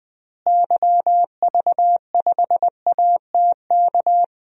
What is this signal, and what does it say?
Morse code, heard as a keyed tone, about 21 wpm, 700 Hz. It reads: YV5ATK
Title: DXCC-100 Call Signs - 6 Characters - Repeated 3x - 20wpm